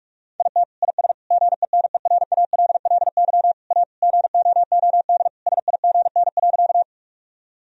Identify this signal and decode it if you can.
Morse code, heard as a keyed tone, about 46 wpm, 700 Hz. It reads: IT IS GENERALLY A GOOD SIGN.
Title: Sentences from Top 1000 Words - Review Part 4 - 45wpm